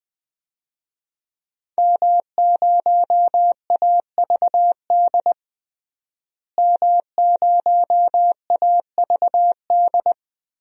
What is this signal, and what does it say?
Morse code, heard as a keyed tone, about 20 wpm, 700 Hz. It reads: M0AVD M0AVD
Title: DXCC-100 Call Signs - 5 Characters - Repeated 2x - 20wpm